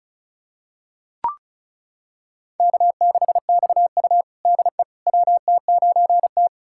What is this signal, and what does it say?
Morse code, heard as a keyed tone, about 36 wpm, 700 Hz. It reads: K6XU DE WT9T
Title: USA Callsign DE Callsign - 4 Characters - 35wpm